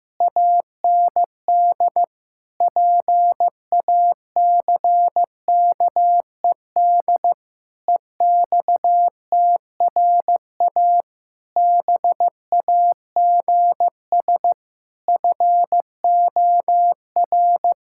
Morse code, heard as a keyed tone, about 15 wpm, 700 Hz. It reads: AND PACKED EXTRA BAGS FOR
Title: Pangram Challenge v2 - 15wpm